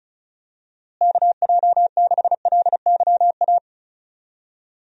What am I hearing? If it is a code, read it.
Morse code, heard as a keyed tone, about 36 wpm, 700 Hz. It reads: KJ6LYA
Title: USA Call Signs - 6 Characters - Repeated 2x - 35wpm